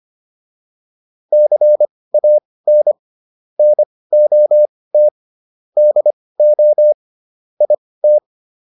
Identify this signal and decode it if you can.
Morse code, heard as a keyed tone, about 25 wpm, spaced as for 18 wpm, 600 Hz. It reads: CAN NOT DO IT
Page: intermediate-practice-files